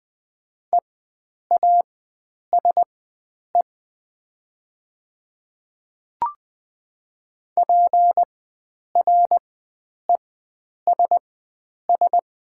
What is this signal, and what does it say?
Morse code, heard as a keyed tone, about 20 wpm, spaced as for 9 wpm, 700 Hz. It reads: EASE PRESS
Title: 4x Character Spacing - Top 500 Words - 20wpm